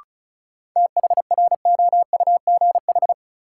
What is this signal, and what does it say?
Morse code, heard as a keyed tone, about 36 wpm, 700 Hz. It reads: THROUGH
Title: Sentences from Top 200 Words - New Word Drill - 35wpm